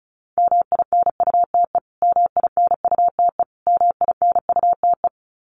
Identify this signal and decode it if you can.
Morse code, heard as a keyed tone, about 34 wpm, 700 Hz. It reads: MINUTE MINUTE MINUTE
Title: Top 500 Words Repeated 35wpm